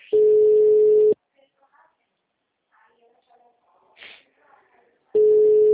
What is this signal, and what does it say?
Ringback tone (Europe, 425 Hz), with speech.
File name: out- -2046- - - .51447